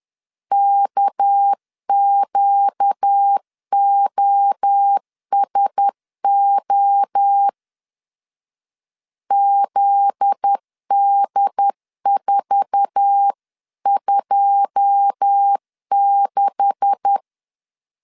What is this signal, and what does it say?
Morse code, heard as a keyed tone, about 10 wpm, 800 Hz. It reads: KQOSO ZD426
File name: CW / DET / 1 9 1500C S A 10wpm 4511